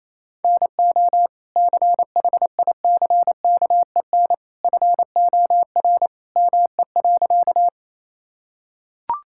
Morse code, heard as a keyed tone, about 28 wpm, 700 Hz. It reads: NO CHICKEN FOR ME.
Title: Sentences from Top 1000 Words - Review Part 5 - 28wpm